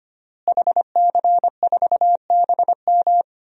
Morse code, heard as a keyed tone, about 25 wpm, 700 Hz. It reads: HC4BM
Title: DXCC-100 Call Signs - 5 Characters - Repeated 3x - 25wpm